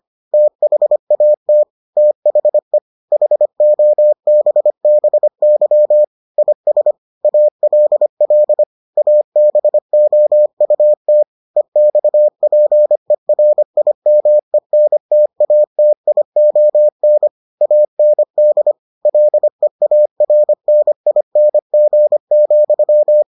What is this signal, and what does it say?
Morse code, heard as a keyed tone, about 25 wpm, 600 Hz. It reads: THAT THE HOBBY IS ALL ABOUT EXPERIMENTATION AND LEARNING,